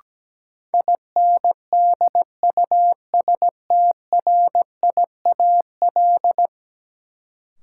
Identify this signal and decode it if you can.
Morse code, heard as a keyed tone, about 17 wpm, 700 Hz. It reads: INDUSTRIAL
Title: Sets of 5 Words - New Word Drill - 17wpm